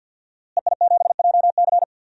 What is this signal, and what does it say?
Morse code, heard as a keyed tone, about 52 wpm, 700 Hz. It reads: EI7JC
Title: DXCC-100 Call Signs - 5 Characters - Repeated 2x - 50wpm